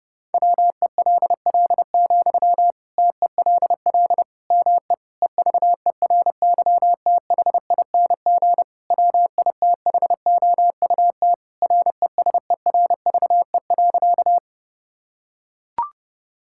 Morse code, heard as a keyed tone, about 30 wpm, 700 Hz. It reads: WELL, TELL ME EVERYTHING WITHOUT RESERVE.
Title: Sentences from Top 1000 Words - Review Part 6 - 30wpm